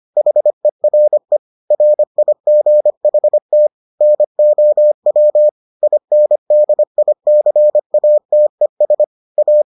Morse code, heard as a keyed tone, about 25 wpm, 600 Hz. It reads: HERE RIGHT NOW INDICATES A